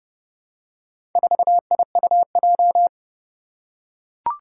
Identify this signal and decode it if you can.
Morse code, heard as a keyed tone, about 30 wpm, 700 Hz. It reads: 4IUJ
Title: Four Challenging Letter-Number Combinations 30wpm